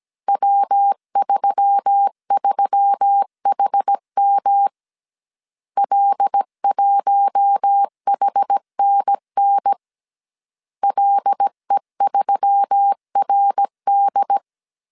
Morse code, heard as a keyed tone, about 17 wpm, 800 Hz. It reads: W33HM L1HNN LE3RD
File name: CW / DET / 1 6 1500C S A 16wpm 2635